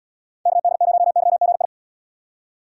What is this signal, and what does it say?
Morse code, heard as a keyed tone, about 69 wpm, 700 Hz. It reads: KD0YGI